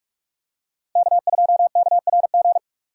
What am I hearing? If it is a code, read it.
Morse code, heard as a keyed tone, about 46 wpm, 700 Hz. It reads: K2KRG
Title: USA Call Signs - 5 Characters - 4x Character Spacing - 45wpm